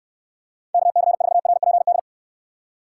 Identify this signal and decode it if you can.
Morse code, heard as a keyed tone, about 66 wpm, 700 Hz. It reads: KC4RPD